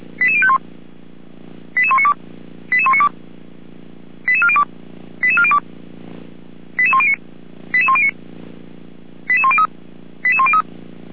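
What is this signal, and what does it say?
Five-tone selective call (ZVEI), heard as quick runs of five tones, tones of about 70 ms. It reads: 88051 88192 88192 88402 88402 88109 88109 88103 88103
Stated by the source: ZVEI five-tone selective call 88102 received on an SDR.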